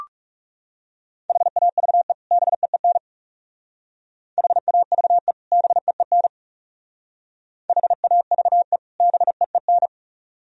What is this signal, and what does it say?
Morse code, heard as a keyed tone, about 39 wpm, 700 Hz. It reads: HAVE BEEN HAVE BEEN HAVE BEEN